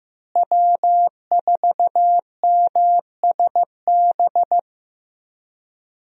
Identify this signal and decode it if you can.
Morse code, heard as a keyed tone, about 16 wpm, 700 Hz. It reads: W4MSB
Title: USA Call Signs - 5 Characters - Repeated 2x - 15wpm